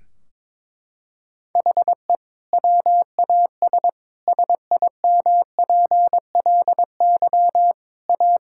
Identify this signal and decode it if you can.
Morse code, heard as a keyed tone, about 21 wpm, 700 Hz. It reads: HE WAS SIMPLY A
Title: Sentences from Top 1600 Words 22wpm